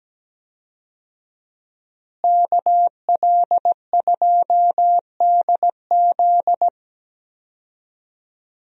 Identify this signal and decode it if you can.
Morse code, heard as a keyed tone, about 17 wpm, 700 Hz. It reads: KL2DZ
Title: USA Call Signs - 5 Characters - Repeated 2x - 17wpm